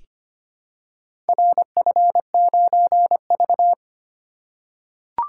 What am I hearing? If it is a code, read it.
Morse code, heard as a keyed tone, about 25 wpm, 700 Hz. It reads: RF9V
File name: practice / 25wpm / Four Letter-Number Combinations - 25wpm